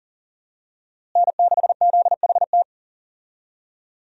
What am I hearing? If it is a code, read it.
Morse code, heard as a keyed tone, about 40 wpm, 700 Hz. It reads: N6ZHT
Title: USA Call Signs - 5 Characters - Repeated 3x - 40wpm